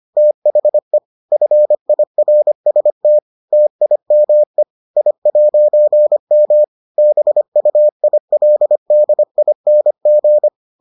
Morse code, heard as a keyed tone, about 25 wpm, 600 Hz. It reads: THE FIRST TIME I'M BUILDING